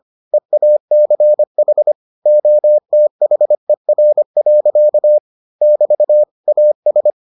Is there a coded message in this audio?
Yes: Morse code from 0.3 to 7.1 s, heard as a keyed tone, about 25 wpm, 600 Hz. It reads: EACH OTHER. <BT> AS